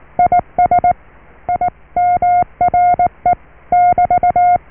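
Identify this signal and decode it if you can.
Morse code, heard as a keyed tone, about 19 wpm, 710 Hz. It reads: IS IMRE <BT>
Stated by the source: Morse QSO on 14 MHz with a German station, received at RST 579.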